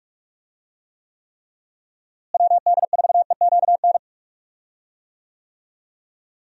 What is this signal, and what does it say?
Morse code, heard as a keyed tone, about 43 wpm, 700 Hz. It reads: WD4EQN
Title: USA Call Signs - 6 Characters - Repeated 3x - 45wpm